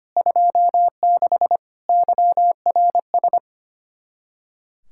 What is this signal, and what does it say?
Morse code, heard as a keyed tone, about 30 wpm, 700 Hz. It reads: 26 YRS
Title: Speed-Racing Common QSO Phrases - 25wpm